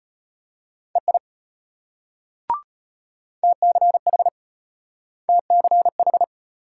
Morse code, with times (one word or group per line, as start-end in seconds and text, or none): 1.0-1.2 EI
3.4-4.3 TCH
5.3-6.2 TCH